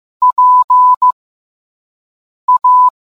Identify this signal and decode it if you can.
Morse code, heard as a keyed tone, about 15 wpm, spaced as for 5 wpm, 1000 Hz. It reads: PA